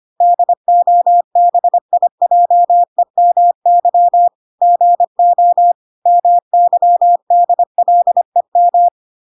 Morse code, heard as a keyed tone, about 25 wpm, 700 Hz. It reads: DOBIJEMY GO MYDLEM